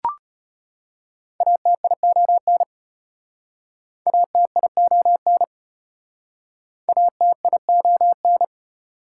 Morse code, heard as a keyed tone, about 34 wpm, 700 Hz. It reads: ATION ATION ATION